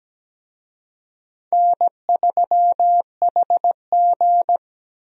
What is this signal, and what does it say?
Morse code, heard as a keyed tone, about 17 wpm, 700 Hz. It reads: N3HG
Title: USA Call Signs - 4 Characters - Repeated 3x - 17wpm